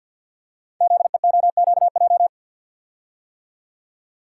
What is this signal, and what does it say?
Morse code, heard as a keyed tone, about 50 wpm, 700 Hz. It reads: ZEOQJ
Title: Code Groups - 5 Letters - 50wpm